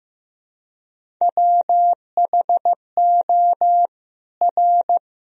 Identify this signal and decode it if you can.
Morse code, heard as a keyed tone, about 15 wpm, 700 Hz. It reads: WHO R
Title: Sentences from Top 100 Words and Abbreviations 15wpm